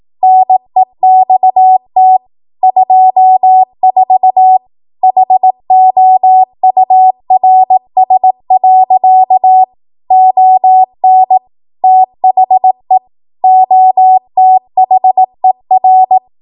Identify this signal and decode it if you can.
Morse code, heard as a keyed tone, about 18 wpm, 750 Hz. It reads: NEXT 24 HOURS. ON THE OTHER
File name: Morse / 18WPM